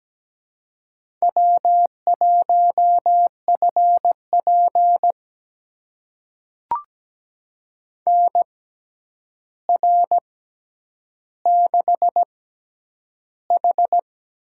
Morse code, heard as a keyed tone, about 17 wpm, spaced as for 9 wpm, 700 Hz. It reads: W1FP NR6H
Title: USA Call Signs - 4 Characters - 6x Character Spacing - 17wpm